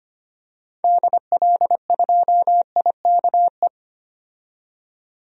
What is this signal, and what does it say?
Morse code, heard as a keyed tone, about 25 wpm, 700 Hz. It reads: DL2IKE
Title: DXCC-100 Call Signs - 6 Characters - 25wpm